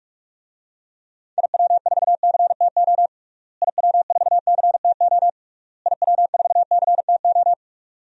Morse code, heard as a keyed tone, about 47 wpm, 700 Hz. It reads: IW4CTO IW4CTO IW4CTO
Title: DXCC-100 Call Signs - 6 Characters - Repeated 3x - 45wpm